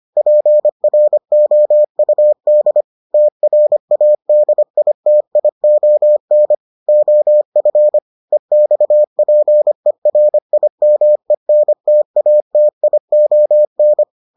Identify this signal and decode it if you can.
Morse code, heard as a keyed tone, about 25 wpm, 600 Hz. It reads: PROUD TRADITION OF EXPERIMENTATION